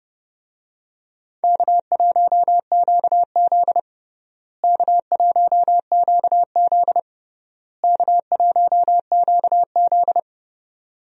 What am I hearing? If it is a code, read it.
Morse code, heard as a keyed tone, about 32 wpm, 700 Hz. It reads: K1QZ K1QZ K1QZ
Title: USA Call Signs - 4 Characters - Repeated 3x - 30wpm